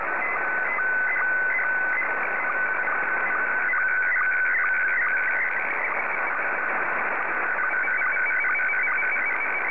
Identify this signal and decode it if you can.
SSTV (slow-scan television), Scottie 1.
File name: sstv / S1-VK7NMH